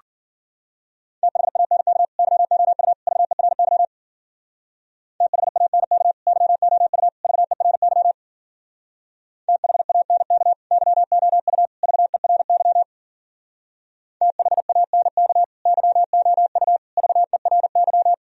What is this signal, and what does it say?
Morse code, heard as a keyed tone, about 52 wpm, 700 Hz. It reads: THANK YOU VERY THANK YOU VERY THANK YOU VERY THANK YOU VERY